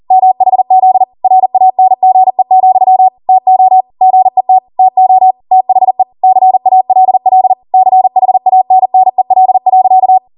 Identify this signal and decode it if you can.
Morse code, heard as a keyed tone, about 40 wpm, 750 Hz. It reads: MHZ RANGE, TO GET TO THE CALL CHANNEL.
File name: Morse / 40WPM